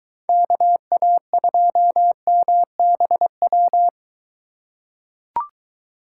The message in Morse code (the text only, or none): KA2MBW